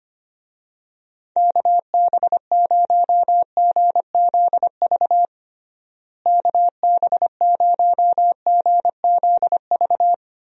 Morse code, heard as a keyed tone, about 25 wpm, 700 Hz. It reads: KB0GZV KB0GZV
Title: USA Call Signs - 6 Characters - Repeated 3x - 25wpm